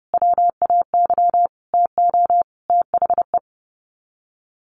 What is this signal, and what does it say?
Morse code, heard as a keyed tone, about 30 wpm, 700 Hz. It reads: WAY TO THE
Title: Sets of 5 Words Encore2 30wpm